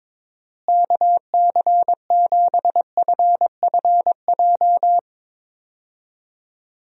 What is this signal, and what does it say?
Morse code, heard as a keyed tone, about 22 wpm, 700 Hz. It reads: KC7FFJ
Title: USA Call Signs - 6 Characters - Repeated 2x - 22wpm